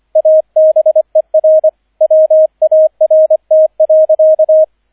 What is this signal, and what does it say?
Morse code, heard as a keyed tone, about 25 wpm, 630 Hz, with radio noise. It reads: ABER WART.